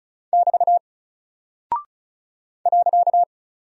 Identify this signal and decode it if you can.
Morse code, heard as a keyed tone, about 36 wpm, 700 Hz. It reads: <BT> .
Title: Single Character (Letter:Number:Punctuation:Prosign) Alt 35wpm